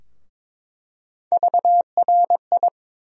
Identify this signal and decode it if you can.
Morse code, heard as a keyed tone, about 21 wpm, 700 Hz. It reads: VRI
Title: USA Call Sign Postfixes - 3 Letters 22wpm